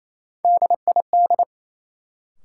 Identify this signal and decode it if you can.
Morse code, heard as a keyed tone, about 28 wpm, 700 Hz. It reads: DID
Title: Sentences from Top 100 Words - New Word Drill - 28wpm